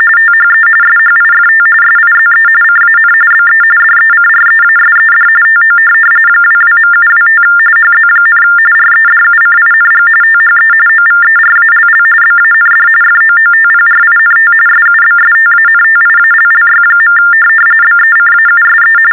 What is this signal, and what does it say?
TTY/TDD text telephone tones, about 45.45 baud. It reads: THE SHIFT SIZE CAN BE DECREASED TO 22.725 HZ, REDUCING THE OVERALL BAND FOOTPRINT SUBSTANTIALLY. BECAUSE
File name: teletype-RTTY-45